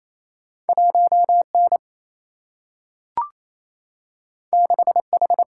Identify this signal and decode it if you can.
Morse code, heard as a keyed tone, about 28 wpm, 700 Hz. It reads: 1N 6H
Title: Single-Letter Single-Number 28wpm